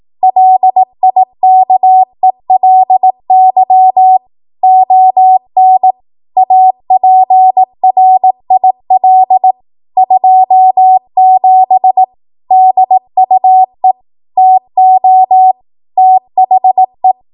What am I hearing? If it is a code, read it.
Morse code, heard as a keyed tone, about 18 wpm, 750 Hz. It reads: LIKELY ON APRIL 27 DUE TO THE
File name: Morse / 18WPM